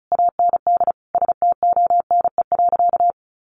Morse code, heard as a keyed tone, about 34 wpm, 700 Hz. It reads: AND STONE.
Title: A Sound Of Thunder 35wpm